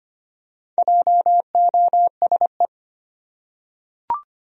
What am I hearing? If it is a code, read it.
Morse code, heard as a keyed tone, about 25 wpm, 700 Hz. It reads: JOSE
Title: USA Common Names - 25wpm